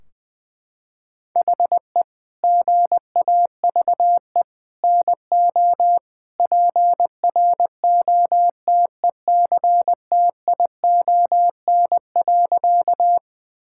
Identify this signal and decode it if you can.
Morse code, heard as a keyed tone, about 20 wpm, 700 Hz. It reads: HE GAVE NO PROTECTION.
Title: Sentences from Top 1600 Words 20wpm